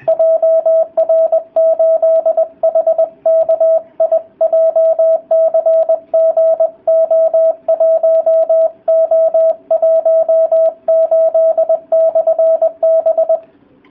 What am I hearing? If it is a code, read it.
Morse code, heard as a keyed tone, about 20 wpm, 650 Hz. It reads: JR8HKI JCG O1O18/B